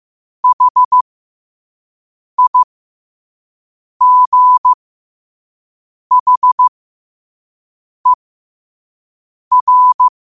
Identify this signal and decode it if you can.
Morse code, heard as a keyed tone, about 15 wpm, spaced as for 5 wpm, 1000 Hz. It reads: HIGHER
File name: qotdfiles / 5wpm